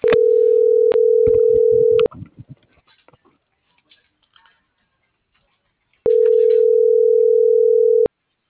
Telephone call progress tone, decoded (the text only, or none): ringback tone